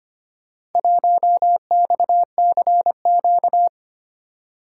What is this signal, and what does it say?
Morse code, heard as a keyed tone, about 25 wpm, 700 Hz. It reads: 1XCQ
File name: practice / 25wpm / Four Letter-Number Combinations - 25wpm